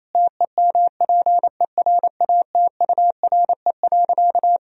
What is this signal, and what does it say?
Morse code, heard as a keyed tone, about 28 wpm, 700 Hz. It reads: TEMPERATURE.
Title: Sentences from Top 1000 Words - Review Part 4 - 28wpm